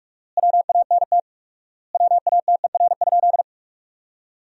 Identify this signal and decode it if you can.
Morse code, heard as a keyed tone, about 47 wpm, 700 Hz. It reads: WANT WATER?
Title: 4x Word-Spacing - Sentences from Top 200 Words - 45wpm